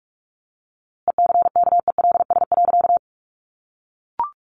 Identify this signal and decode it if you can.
Morse code, heard as a keyed tone, about 44 wpm, 700 Hz. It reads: ECKELS.
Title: A Sound Of Thunder 45wpm